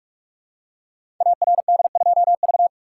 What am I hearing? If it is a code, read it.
Morse code, heard as a keyed tone, about 44 wpm, 700 Hz. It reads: ARD2V
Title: Code Groups - 5 Letter:Numbers - 45wpm